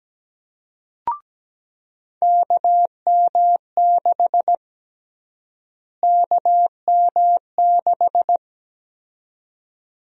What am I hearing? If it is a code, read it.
Morse code, heard as a keyed tone, about 17 wpm, 700 Hz. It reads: KM6 KM6
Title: USA Call Sign Prefixes - Repeated 2x - 17wpm